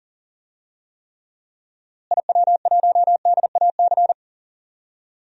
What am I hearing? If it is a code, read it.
Morse code, heard as a keyed tone, about 38 wpm, 700 Hz. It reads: IW1DAC
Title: DXCC-100 Call Signs - 6 Characters - Repeated 2x - 40wpm